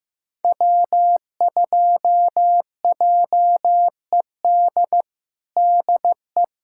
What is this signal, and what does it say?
Morse code, heard as a keyed tone, about 15 wpm, 700 Hz. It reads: W2JED DE
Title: USA Callsign DE Callsign - 5 Characters - 15wpm